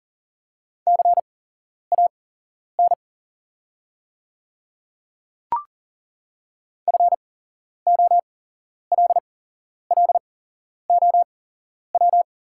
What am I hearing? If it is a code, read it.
Morse code, heard as a keyed tone, about 40 wpm, spaced as for 11 wpm, 700 Hz. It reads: CAN FOLLOW
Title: 8x Character Spacing - Top 300 Words - 40wpm